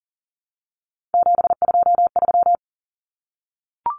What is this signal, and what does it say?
Morse code, heard as a keyed tone, about 37 wpm, 700 Hz. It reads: 723
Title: Numbers - 3-Digit 40wpm